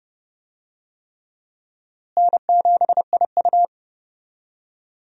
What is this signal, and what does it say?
Morse code, heard as a keyed tone, about 30 wpm, 700 Hz. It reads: N7IU
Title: USA Call Signs - 4 Characters - Repeated 3x - 30wpm